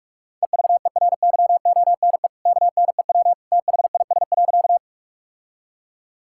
Morse code, heard as a keyed tone, about 43 wpm, 700 Hz. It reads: EVERYONE KNEW THIS.
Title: Sentences from Top 1000 Words - Review Part 5 - 45wpm